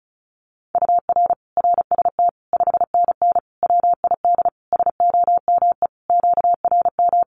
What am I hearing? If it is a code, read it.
Morse code, heard as a keyed tone, about 34 wpm, 700 Hz. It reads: UR RST 5NN WID SOME QRM
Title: Common QSO Phrases 35wpm